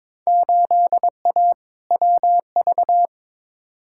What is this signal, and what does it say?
Morse code, heard as a keyed tone, about 21 wpm, 700 Hz. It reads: 8A WV
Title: Field Day Exchanges Encore - 22wpm